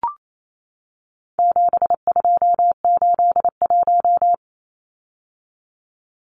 Morse code, heard as a keyed tone, about 28 wpm, 700 Hz. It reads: 7281
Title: Numbers - 4-Digit 28wpm